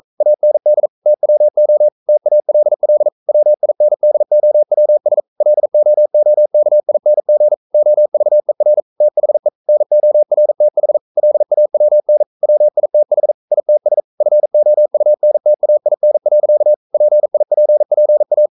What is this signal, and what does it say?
Morse code, heard as a keyed tone, about 42 wpm, 600 Hz. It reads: AND TWO TALL WINDOWS LOOKING OVER THE NORTH LAWN WITH ITS FOUNTAIN. PIPPA
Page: Romanen Never af Ken Follet på engelsk.